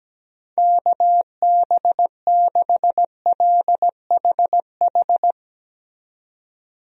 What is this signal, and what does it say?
Morse code, heard as a keyed tone, about 17 wpm, 700 Hz. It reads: KB6LHH
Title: USA Call Signs - 6 Characters - 17wpm